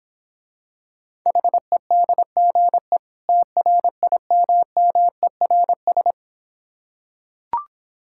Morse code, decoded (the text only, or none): HEDGE TRIMMERS